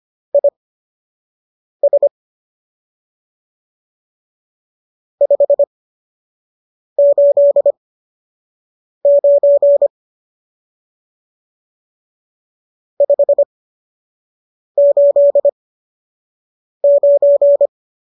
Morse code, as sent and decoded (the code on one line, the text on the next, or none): .. ... / ..... ---.. ----. / ..... ---.. ----.
IS 589 589